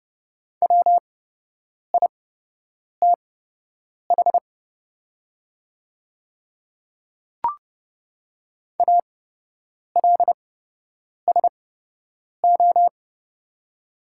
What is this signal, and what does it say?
Morse code, heard as a keyed tone, about 30 wpm, spaced as for 8 wpm, 700 Hz. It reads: WITH ALSO
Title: 8x Character Spacing - Top 200 Words - 30wpm